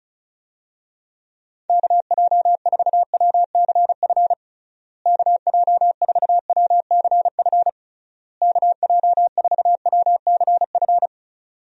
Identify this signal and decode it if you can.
Morse code, heard as a keyed tone, about 34 wpm, 700 Hz. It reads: KJ4WCF KJ4WCF KJ4WCF
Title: USA Call Signs - 6 Characters - Repeated 3x - 35wpm